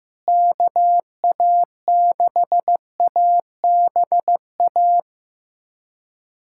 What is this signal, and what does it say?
Morse code, heard as a keyed tone, about 15 wpm, 700 Hz. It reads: KA6ABA
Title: USA Callsign DE Callsign - 6 Characters - 15wpm